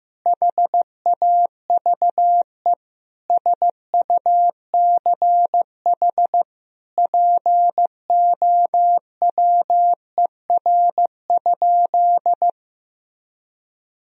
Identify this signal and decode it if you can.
Morse code, heard as a keyed tone, about 15 wpm, 700 Hz. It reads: HAVE SUCH POWER?
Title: Sentences from Top 1000 Words - Review Part 2 - 15wpm